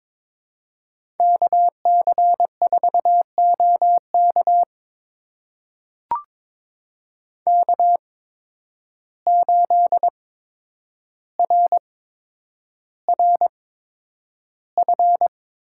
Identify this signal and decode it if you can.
Morse code, heard as a keyed tone, about 23 wpm, spaced as for 10 wpm, 700 Hz. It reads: KC4OK K8RRF
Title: USA Call Signs - 5 Characters - 8x Character Spacing - 22wpm